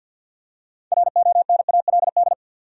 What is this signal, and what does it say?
Morse code, heard as a keyed tone, about 53 wpm, 700 Hz. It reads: RONALD
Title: USA Common Names - 50wpm